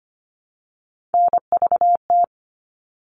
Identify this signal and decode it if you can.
Morse code, heard as a keyed tone, about 25 wpm, 700 Hz. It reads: NVT
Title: All Cut Numbers - 3-Digit 25wpm